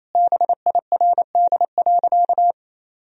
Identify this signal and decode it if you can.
Morse code, heard as a keyed tone, about 28 wpm, 700 Hz. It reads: BIRD.
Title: Sentences from Top 1000 Words - Review Part 2 - 28wpm